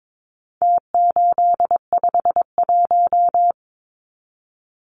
Morse code, heard as a keyed tone, about 22 wpm, 700 Hz. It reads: T851
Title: Numbers - Contest Style - 4-Digit 22wpm